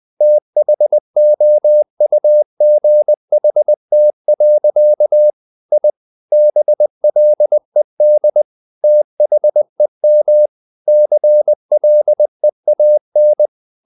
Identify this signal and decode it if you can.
Morse code, heard as a keyed tone, about 20 wpm, 600 Hz. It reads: THOUGHT. I BLED THEM CLEAN
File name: the old man and the sea part 2 20wpm